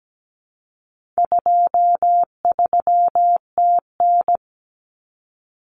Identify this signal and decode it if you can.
Morse code, heard as a keyed tone, about 17 wpm, 700 Hz. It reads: 23TN
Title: Common Cut Numbers - 4-Digit 17wpm